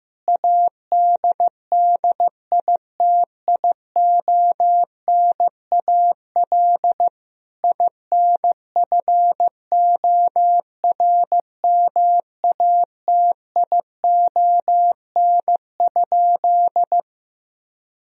Morse code, heard as a keyed tone, about 16 wpm, 700 Hz. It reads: ADDITIONAL INFORMATION?
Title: Sentences from Top 1000 Words - Review Part 4 - 15wpm